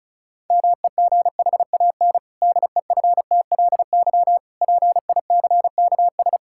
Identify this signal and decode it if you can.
Morse code, heard as a keyed tone, about 34 wpm, 700 Hz. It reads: MEGHAN DEFTLY PICKS
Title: Pangram Challenge v2 - 35wpm